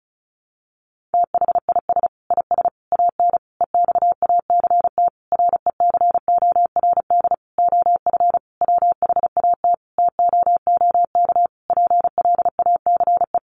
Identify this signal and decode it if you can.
Morse code, heard as a keyed tone, about 34 wpm, 700 Hz. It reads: THIS IS AN EXACT RECORD OF WHAT TOOK PLACE
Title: Sentences from Top 1300 Words 35wpm